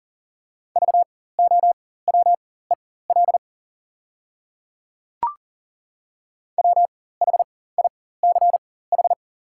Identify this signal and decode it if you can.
Morse code, heard as a keyed tone, about 40 wpm, spaced as for 19 wpm, 700 Hz. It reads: VOWEL WHICH
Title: 4x Character Spacing - Top 500 Words - 40wpm